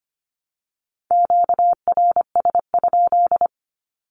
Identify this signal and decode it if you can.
Morse code, heard as a keyed tone, about 25 wpm, 700 Hz. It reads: QRS?
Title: Q-Codes 25wpm